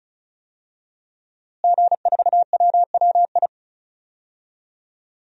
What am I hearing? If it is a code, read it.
Morse code, heard as a keyed tone, about 36 wpm, 700 Hz. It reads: G4WWI
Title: DXCC-100 Call Signs - 5 Characters - Repeated 3x - 35wpm